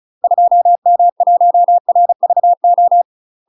Morse code, heard as a keyed tone, about 36 wpm, 700 Hz. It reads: 2M1RVO
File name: UK MASTER QSOs TRAINING / 35WPM / UK MASTER QSOs 0009